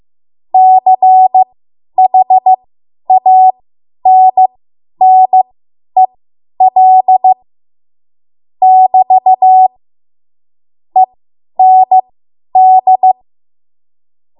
Morse code, heard as a keyed tone, about 15 wpm, spaced as for 10 wpm, 750 Hz. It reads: CHANNEL <BT> END
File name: Morse / 10WPM